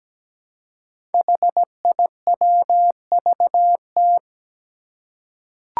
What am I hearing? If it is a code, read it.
Morse code, heard as a keyed tone, about 17 wpm, 700 Hz. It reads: HIWVT
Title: Five Challenging Letter-Number Combinations 17wpm